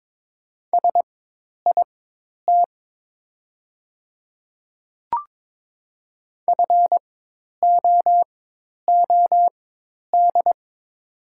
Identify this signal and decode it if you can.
Morse code, heard as a keyed tone, about 22 wpm, spaced as for 10 wpm, 700 Hz. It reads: SIT FOOD
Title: 4x Character Spacing - Top 500 Words - 22wpm